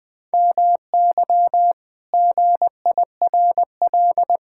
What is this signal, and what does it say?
Morse code, heard as a keyed tone, about 19 wpm, 700 Hz. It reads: MY GIRL
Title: Pangram Challenge v2 - 20wpm